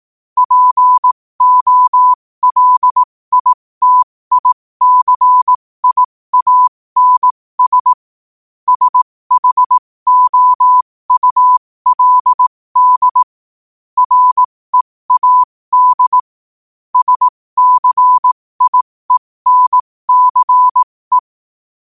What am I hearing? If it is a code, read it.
Morse code, heard as a keyed tone, about 18 wpm, spaced as for 15 wpm, 1000 Hz. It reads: POLITICIANS SHOULD READ SCIENCE
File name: qotdfiles / 15wpm